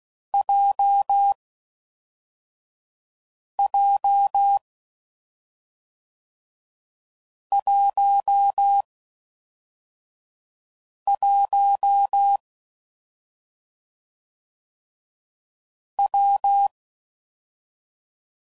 Morse code, heard as a keyed tone, about 16 wpm, 800 Hz. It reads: J J 1 1 W
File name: Combinations / Difficult Combos / WPJ WPJ1